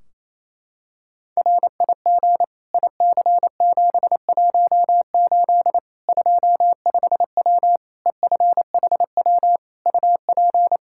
Morse code, heard as a keyed tone, about 28 wpm, 700 Hz. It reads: RIG IC718 25W EFHW UP
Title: Practice QSO - Exchange 2 - Info - 28wpm